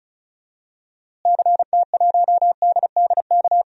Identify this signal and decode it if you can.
Morse code, heard as a keyed tone, about 36 wpm, 700 Hz. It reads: CT1DDK
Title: DXCC-100 Call Signs - 6 Characters - Repeated 3x - 35wpm